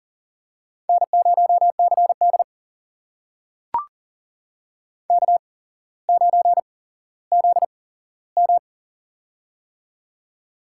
Morse code, heard as a keyed tone, about 40 wpm, spaced as for 17 wpm, 700 Hz. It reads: N0CD K9ZM
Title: USA Call Signs - 4 Characters - 8x Character Spacing - 40wpm